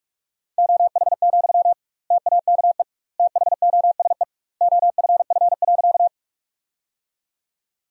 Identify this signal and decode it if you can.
Morse code, heard as a keyed tone, about 46 wpm, 700 Hz. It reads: OH, TAKE THOSE OFF.
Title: 2x Word-Spacing - Sentences from Top 500 Words - 45wpm